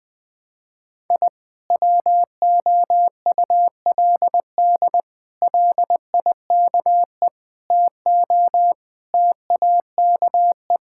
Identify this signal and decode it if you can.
Morse code, heard as a keyed tone, about 20 wpm, 700 Hz. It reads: I WOULD LIKE TO TAKE
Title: Sets of 8 Words - 20wpm